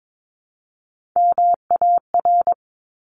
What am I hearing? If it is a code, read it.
Morse code, heard as a keyed tone, about 22 wpm, 700 Hz. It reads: MAR
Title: ARRL Sections 22wpm